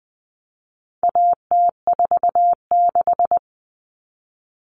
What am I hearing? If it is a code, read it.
Morse code, heard as a keyed tone, about 20 wpm, 700 Hz. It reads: AT46
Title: All Cut Numbers - 4-Digit 20wpm